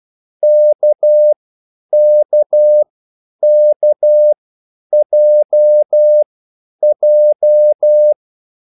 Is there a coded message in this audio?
Yes: Morse code from 0.4 to 8.1 s, heard as a keyed tone, about 12 wpm, spaced as for 9 wpm, 600 Hz. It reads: KKKJJ